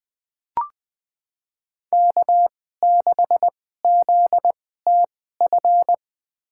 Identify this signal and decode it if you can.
Morse code, heard as a keyed tone, about 20 wpm, spaced as for 14 wpm, 700 Hz. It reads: K6ZTF
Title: USA Call Signs - 5 Characters - 2x Character Spacing - 20wpm